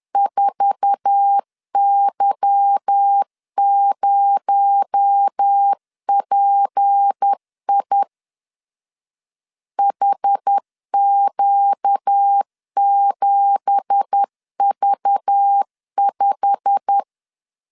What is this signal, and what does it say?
Morse code, heard as a keyed tone, about 10 wpm, 800 Hz. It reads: 4Y0PI HQ7V5
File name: LEZIONI / 10 SIGNS / 10A SLASH 10wpm 2808